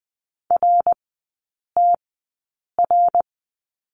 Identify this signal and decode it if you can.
Morse code, heard as a keyed tone, about 20 wpm, spaced as for 8 wpm, 700 Hz. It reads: RTR
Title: 053 Lesson 10e - Letter r - Three Characters 20wpm